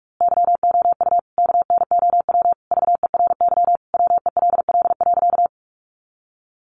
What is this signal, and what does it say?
Morse code, heard as a keyed tone, about 44 wpm, 700 Hz. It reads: YOU KNOW VERY WELL.
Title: A Sound Of Thunder 45wpm